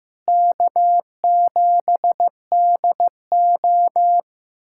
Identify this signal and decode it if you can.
Morse code, heard as a keyed tone, about 16 wpm, 700 Hz. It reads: K7DO
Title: USA Callsign DE Callsign - 4 Characters - 15wpm